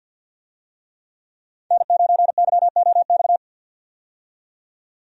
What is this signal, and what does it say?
Morse code, heard as a keyed tone, about 51 wpm, 700 Hz. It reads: N9YOX
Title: USA Call Signs - 5 Characters - 50wpm